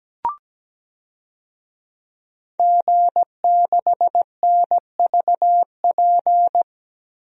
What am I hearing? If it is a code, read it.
Morse code, heard as a keyed tone, about 17 wpm, 700 Hz. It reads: G6NVP
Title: DXCC-100 Call Signs - 5 Characters - 17wpm